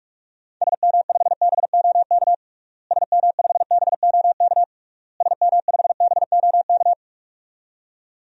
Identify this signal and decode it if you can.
Morse code, heard as a keyed tone, about 46 wpm, 700 Hz. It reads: SM5BOK SM5BOK SM5BOK
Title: DXCC-100 Call Signs - 6 Characters - Repeated 3x - 45wpm